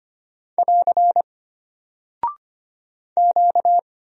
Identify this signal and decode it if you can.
Morse code, heard as a keyed tone, about 25 wpm, 700 Hz. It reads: <AR> Q
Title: Single Character (Letter:Number:Punctuation:Prosign) Alt 25wpm